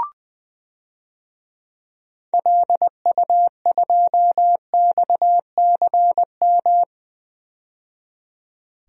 Morse code, heard as a keyed tone, about 20 wpm, 700 Hz. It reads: LU2XCM
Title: DXCC-100 Call Signs - 6 Characters - 20wpm